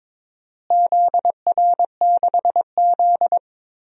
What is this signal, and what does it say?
Morse code, heard as a keyed tone, about 22 wpm, 700 Hz. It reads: ZR6Z
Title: DXCC-100 Call Signs - 4 Characters - Repeated 3x - 22wpm